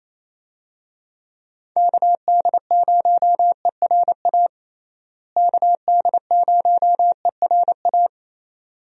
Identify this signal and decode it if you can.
Morse code, heard as a keyed tone, about 28 wpm, 700 Hz. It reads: KD0ERA KD0ERA
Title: USA Call Signs - 6 Characters - Repeated 2x - 28wpm